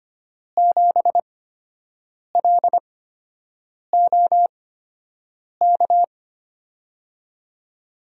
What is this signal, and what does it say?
Morse code, heard as a keyed tone, about 25 wpm, spaced as for 7 wpm, 700 Hz. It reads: 7LOK
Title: USA Call Signs - 6 Characters - 8x Character Spacing - 25wpm